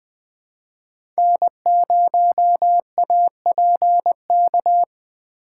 Morse code, heard as a keyed tone, about 20 wpm, 700 Hz. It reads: N0APK
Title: USA Call Signs - 5 Characters - 2x Character Spacing - 20wpm